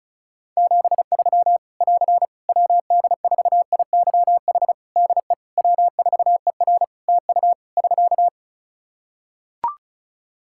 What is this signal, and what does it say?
Morse code, heard as a keyed tone, about 36 wpm, 700 Hz. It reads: 73 <AR> WD4IYH DE W4ER TU <SK>
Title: Practice QSO - Exchange 4 - Ending - 35wpm